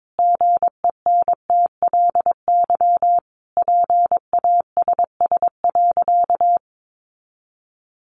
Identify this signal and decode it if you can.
Morse code, heard as a keyed tone, about 22 wpm, 700 Hz. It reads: GENTLY PASS.
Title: Sentences from Top 1500 Words 22wpm